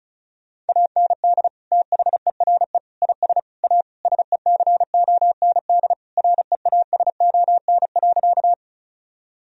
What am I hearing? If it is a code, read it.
Morse code, heard as a keyed tone, about 36 wpm, 700 Hz. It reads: AND THERE IS A SECOND REASON.
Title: Sentences from Top 1000 Words - Review Part 1 - 35wpm